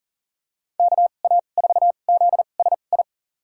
Morse code, heard as a keyed tone, about 40 wpm, spaced as for 29 wpm, 700 Hz. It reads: KA4ZSI
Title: USA Call Signs - 6 Characters - 2x Character Spacing - 40wpm